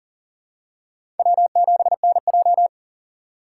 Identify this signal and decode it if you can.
Morse code, heard as a keyed tone, about 40 wpm, 700 Hz. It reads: W7NJ
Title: USA Call Signs - 4 Characters - 6x Character Spacing - 40wpm